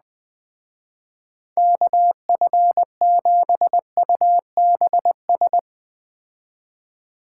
Morse code, heard as a keyed tone, about 21 wpm, 700 Hz. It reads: KF7UBS
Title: Speed Racing - USA 2x3 Call Signs 17wpm